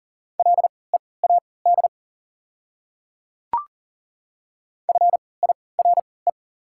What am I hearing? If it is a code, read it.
Morse code, heard as a keyed tone, about 40 wpm, spaced as for 23 wpm, 700 Hz. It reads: LEAD FIRE
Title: 3x Character Spacing - Top 500 Words - 40wpm